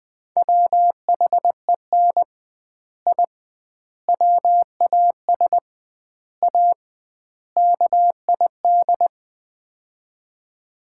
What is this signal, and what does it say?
Morse code, heard as a keyed tone, about 20 wpm, 700 Hz. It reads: WHEN I WAS A KID
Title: 2x Word-Spacing - Sets of 5 Words - 20wpm